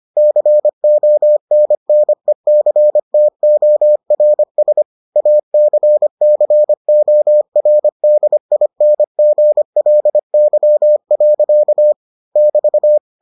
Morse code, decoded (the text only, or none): CONNECTORS ACCORDINGLY. <BT>